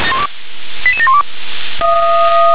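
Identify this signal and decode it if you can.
Five-tone selective call (ZVEI), heard as quick runs of five tones, tones of about 70 ms. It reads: 88712
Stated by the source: ZVEI five-tone call received by SDR.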